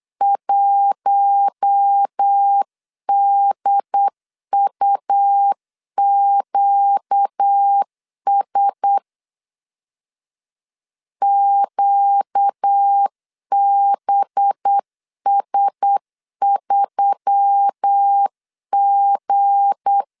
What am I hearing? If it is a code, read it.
Morse code, heard as a keyed tone, about 8 wpm, 800 Hz. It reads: 1DUQS QBS3G
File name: LEZIONI / 7 GF9Z / 7 G 8wpm 3118